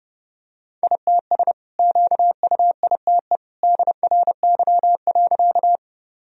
Morse code, heard as a keyed tone, about 30 wpm, 700 Hz. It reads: ITS QUITE DRY.
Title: Sentences from Top 1000 Words - Review Part 3 - 30wpm